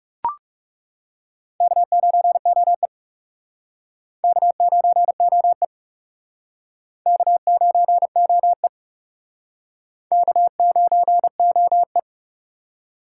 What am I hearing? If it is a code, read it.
Morse code, heard as a keyed tone, about 37 wpm, 700 Hz. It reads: K9OE K9OE K9OE K9OE
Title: Speed Racing - USA 1x2 Call Signs 30wpm